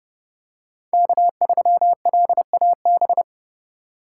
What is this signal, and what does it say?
Morse code, heard as a keyed tone, about 30 wpm, 700 Hz. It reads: K3LAB
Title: USA Call Signs - 5 Characters - 4x Character Spacing - 30wpm